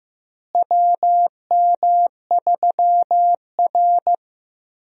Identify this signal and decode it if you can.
Morse code, heard as a keyed tone, about 15 wpm, 700 Hz. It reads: WM3R
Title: USA Call Signs - 4 Characters - Repeated 3x - 15wpm